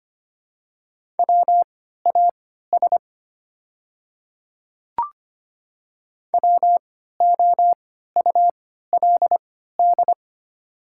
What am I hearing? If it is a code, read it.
Morse code, heard as a keyed tone, about 25 wpm, spaced as for 14 wpm, 700 Hz. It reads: WAS WOULD
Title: 3x Character Spacing - Top 100 Words - 25wpm